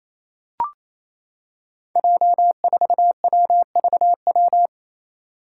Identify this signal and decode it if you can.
Morse code, heard as a keyed tone, about 28 wpm, 700 Hz. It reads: J4WVW
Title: Five Challenging Letter-Number Combinations 28wpm